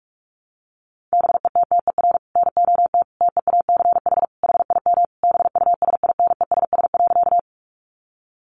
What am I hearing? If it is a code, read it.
Morse code, heard as a keyed tone, about 44 wpm, 700 Hz. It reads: BETTER NOT TEACH HIM BUSINESS.
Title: Sentences from Top 600 Words 45wpm